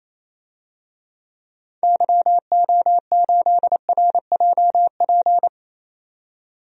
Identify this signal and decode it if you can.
Morse code, heard as a keyed tone, about 28 wpm, 700 Hz. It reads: YO8RJP
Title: DXCC-100 Call Signs - 6 Characters - 28wpm